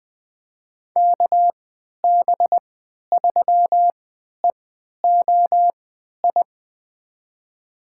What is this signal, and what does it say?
Morse code, heard as a keyed tone, about 20 wpm, spaced as for 11 wpm, 700 Hz. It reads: KB3EOI
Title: USA Call Signs - 6 Characters - 3x Character Spacing - 20wpm